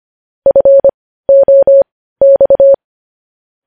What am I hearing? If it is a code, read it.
Morse code, heard as a keyed tone, about 25 wpm, spaced as for 15 wpm, 550 Hz. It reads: FOX
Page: News headlines at 15